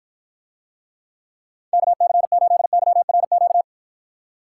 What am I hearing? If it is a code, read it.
Morse code, heard as a keyed tone, about 53 wpm, 700 Hz. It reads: KC8YRQ